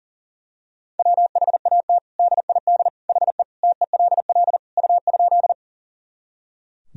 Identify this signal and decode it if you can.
Morse code, heard as a keyed tone, about 40 wpm, 700 Hz. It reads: WHAT DID HE TELL U?
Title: Sentences from Top 200 Words and Abbreviations 40wpm